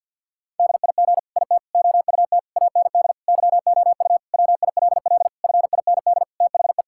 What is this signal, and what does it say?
Morse code, heard as a keyed tone, about 50 wpm, 700 Hz. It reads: DIG IT OUT AND YOU WILL FIND THE
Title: Sentences from Top 1600 Words 50wpm